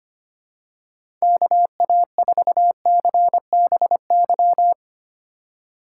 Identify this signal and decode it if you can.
Morse code, heard as a keyed tone, about 25 wpm, 700 Hz. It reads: KA4CBY
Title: USA Call Signs - 6 Characters - 2x Character Spacing - 25wpm